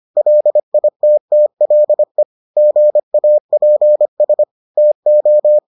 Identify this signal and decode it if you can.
Morse code, heard as a keyed tone, about 25 wpm, 600 Hz. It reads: LITTLE GAPS TO